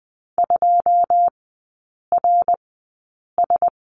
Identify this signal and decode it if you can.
Morse code, heard as a keyed tone, about 20 wpm, spaced as for 8 wpm, 700 Hz. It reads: 2RS
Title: 093 Lesson 16e - Letter c - Three Characters 20wpm